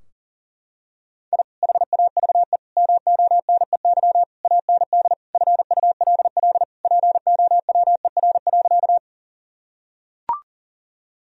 Morse code, heard as a keyed tone, about 40 wpm, 700 Hz. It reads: I HAVE MONEY AND FULL POWER.
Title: Sentences from Top 1000 Words - Review Part 3 - 40wpm